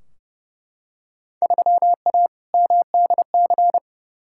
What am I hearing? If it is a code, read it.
Morse code, heard as a keyed tone, about 30 wpm, 700 Hz. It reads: 3A MDC
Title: Field Day Exchanges - 30wpm